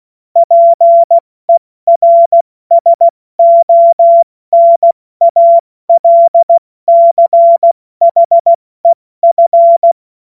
Morse code, heard as a keyed tone, about 16 wpm, 680 Hz. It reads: PERSONALCHEF
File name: Morsekurs / Witze / Witz 11. April 70 F80